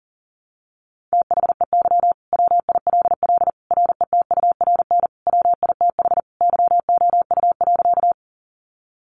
Morse code, heard as a keyed tone, about 40 wpm, 700 Hz. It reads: THEY WILL RETURN WITH YOU.
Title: Sentences from Top 800 Words 40wpm